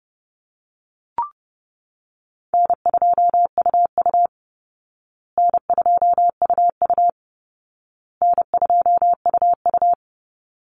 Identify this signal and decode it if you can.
Morse code, heard as a keyed tone, about 30 wpm, 700 Hz. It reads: N2UU N2UU N2UU
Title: Call Signs Repeated 2x - 30wpm